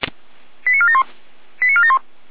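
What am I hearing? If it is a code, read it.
Five-tone selective call (ZVEI), heard as quick runs of five tones, tones of about 70 ms. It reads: 89471 89471
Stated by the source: ZVEI five-tone selective call received on an SDR.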